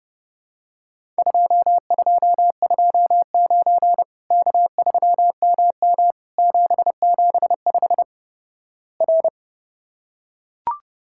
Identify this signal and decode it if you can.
Morse code, heard as a keyed tone, about 30 wpm, 700 Hz. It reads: 2229 K3MM 775 R
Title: Worked All Europe DX Contest - QTC Exchange - 30wpm